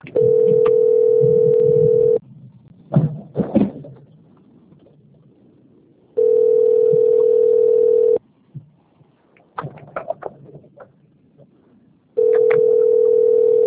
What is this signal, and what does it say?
Ringback tone (North America), 440 + 480 Hz.